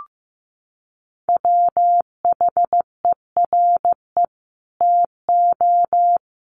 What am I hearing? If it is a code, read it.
Morse code, heard as a keyed tone, about 15 wpm, 700 Hz. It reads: WHERE TO
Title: Sentences from Top 1400 Words 15wpm